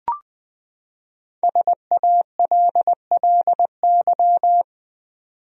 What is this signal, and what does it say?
Morse code, heard as a keyed tone, about 20 wpm, 700 Hz. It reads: SALLY
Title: USA Common Names - Max 5 Letters - 20wpm